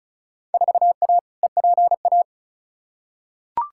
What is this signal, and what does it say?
Morse code, heard as a keyed tone, about 34 wpm, 700 Hz. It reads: 4A EPA
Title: Field Day Exchanges - 35wpm